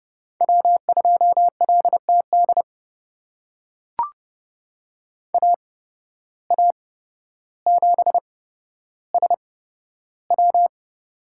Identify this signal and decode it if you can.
Morse code, heard as a keyed tone, about 31 wpm, spaced as for 13 wpm, 700 Hz. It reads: W2LTD AA7SW
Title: USA Call Signs - 5 Characters - 8x Character Spacing - 30wpm